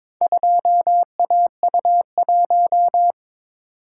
Morse code, heard as a keyed tone, about 22 wpm, 700 Hz. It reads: 2AU1
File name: practice / 22wpm / Four Letter-Number Combinations - 22wpm